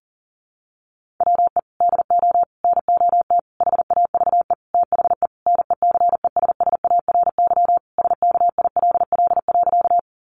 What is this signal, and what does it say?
Morse code, heard as a keyed tone, about 37 wpm, 700 Hz. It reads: WE DO NOT HAVE THE NECESSARY SKILL.
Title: Sentences from Top 1200 Words 40wpm